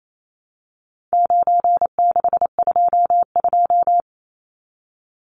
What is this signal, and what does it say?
Morse code, heard as a keyed tone, about 28 wpm, 700 Hz. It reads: 9622
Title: Numbers - 4-Digit 28wpm